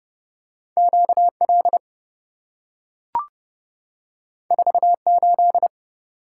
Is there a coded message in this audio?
Yes: Morse code from 0.8 to 5.7 s, heard as a keyed tone, about 27 wpm, 700 Hz. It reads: QL 48